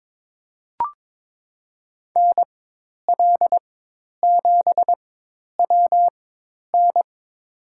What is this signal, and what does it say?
Morse code, heard as a keyed tone, about 22 wpm, spaced as for 10 wpm, 700 Hz. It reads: NL7WN
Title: USA Call Signs - 5 Characters - 4x Character Spacing - 22wpm